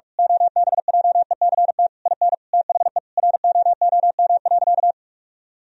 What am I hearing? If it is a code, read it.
Morse code, heard as a keyed tone, about 46 wpm, 700 Hz. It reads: OBJECT IN THE ROOM.
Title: Sentences from Top 1000 Words - Review Part 3 - 45wpm